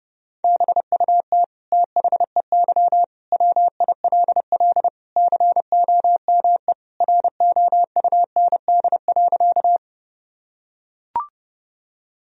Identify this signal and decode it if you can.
Morse code, heard as a keyed tone, about 30 wpm, 700 Hz. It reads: BUT THEY WILL COME ROUND.
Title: Sentences from Top 1000 Words - Review Part 1 - 30wpm